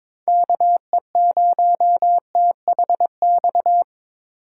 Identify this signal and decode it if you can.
Morse code, heard as a keyed tone, about 22 wpm, 700 Hz. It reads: KE0THX
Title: USA Call Signs - 6 Characters - 3x Character Spacing - 22wpm